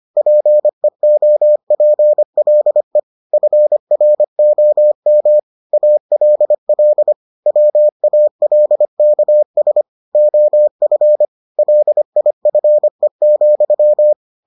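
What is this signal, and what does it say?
Morse code, heard as a keyed tone, about 25 wpm, 600 Hz. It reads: PEOPLE FROM ALL WALKS OF LIFE,
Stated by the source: Morse edition of the Foundations of Amateur Radio episodes.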